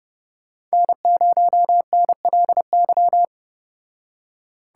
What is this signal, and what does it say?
Morse code, heard as a keyed tone, about 34 wpm, 700 Hz. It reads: N0NLY